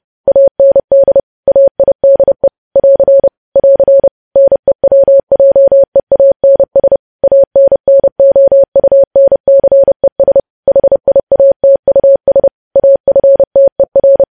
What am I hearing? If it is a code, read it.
Morse code, heard as a keyed tone, about 30 wpm, 550 Hz. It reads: AND AIDE <AR> <AR> NEWJEANS ANNOUNCES HIATUS AFTER
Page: news headlines in morse code at 30 wpm